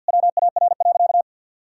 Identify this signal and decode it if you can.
Morse code, heard as a keyed tone, about 52 wpm, 700 Hz. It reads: WAR.
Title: Sentences from Top 1000 Words - Review Part 6 - 50wpm